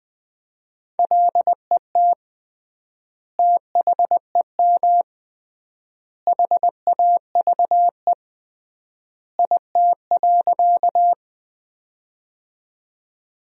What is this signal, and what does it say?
Morse code, heard as a keyed tone, about 21 wpm, 700 Hz. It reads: LET THEM HAVE IT.
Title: 3x Word-Spacing - Sentences from Top 300 Words - 20wpm